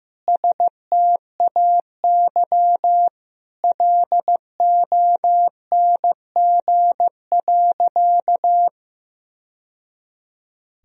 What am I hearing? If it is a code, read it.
Morse code, heard as a keyed tone, about 15 wpm, 700 Hz. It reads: STAY LONG.
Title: Sentences from Top 1000 Words - Review Part 3 - 15wpm